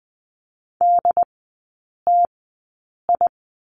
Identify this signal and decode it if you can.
Morse code, heard as a keyed tone, about 20 wpm, spaced as for 8 wpm, 700 Hz. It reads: DTI
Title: 093 Lesson 16e - Letter c - Three Characters 20wpm